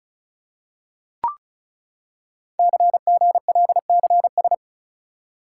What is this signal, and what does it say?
Morse code, heard as a keyed tone, about 36 wpm, 700 Hz. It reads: CGLCS
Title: Code Groups - 5 Letters - 35wpm